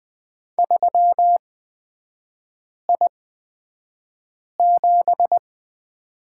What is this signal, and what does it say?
Morse code, heard as a keyed tone, about 20 wpm, spaced as for 5 wpm, 700 Hz. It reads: 3I7
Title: 224 Lesson 36e - BT - Three Characters 20wpm